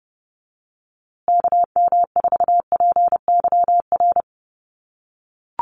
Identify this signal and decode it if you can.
Morse code, heard as a keyed tone, about 30 wpm, 700 Hz. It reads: KM4PYR
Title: USA Call Signs Repeated 2x 30wpm